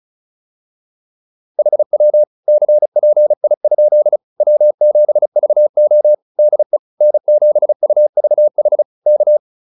Morse code, heard as a keyed tone, about 35 wpm, 600 Hz. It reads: HW CPI? W7VO DE N7UVH K
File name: QSO105 35